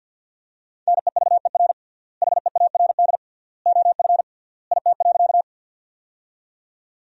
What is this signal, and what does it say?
Morse code, heard as a keyed tone, about 50 wpm, 700 Hz. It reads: NEVER HEARD OF IT.
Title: 3x Word-Spacing - Sentences from Top 400 Words - 50wpm